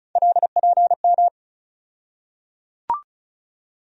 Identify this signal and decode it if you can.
Morse code, heard as a keyed tone, about 34 wpm, 700 Hz. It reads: LPM
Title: USA Call Sign Postfixes - 3 Letters 35wpm